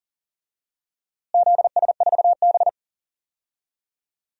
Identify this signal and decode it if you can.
Morse code, heard as a keyed tone, about 40 wpm, 700 Hz. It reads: ZS4B
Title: DXCC-100 Call Signs - 4 Characters - Repeated 2x - 40wpm